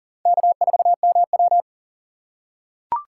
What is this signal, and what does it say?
Morse code, heard as a keyed tone, about 40 wpm, 700 Hz. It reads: K4MW
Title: USA Call Signs - 2x Character Spacing - 40wpm